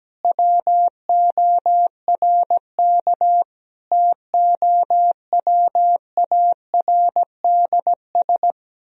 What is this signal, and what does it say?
Morse code, heard as a keyed tone, about 17 wpm, 700 Hz. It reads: WORK TOWARDS
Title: Sentences from Top 1000 Words - Review Part 3 - 17wpm